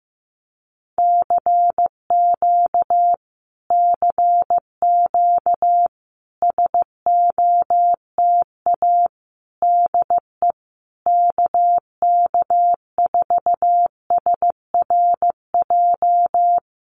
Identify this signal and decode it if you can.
Morse code, heard as a keyed tone, about 15 wpm, 700 Hz. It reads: CQ CQ SOTA DE KK4SRJ
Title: Common QSO Phrases Encore 15wpm